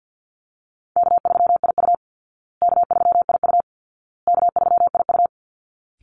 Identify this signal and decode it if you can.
Morse code, heard as a keyed tone, about 51 wpm, 700 Hz. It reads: K3IU K3IU K3IU
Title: Call Signs Repeated 2x - 50wpm